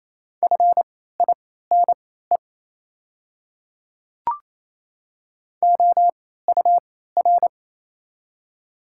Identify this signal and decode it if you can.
Morse code, heard as a keyed tone, about 28 wpm, spaced as for 16 wpm, 700 Hz. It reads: FINE OUR
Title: 3x Character Spacing - Top 500 Words - 28wpm